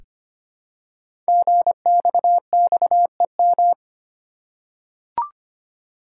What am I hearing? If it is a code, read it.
Morse code, heard as a keyed tone, about 25 wpm, 700 Hz. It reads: GXXEM
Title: Code Groups - 5 Letters - 25wpm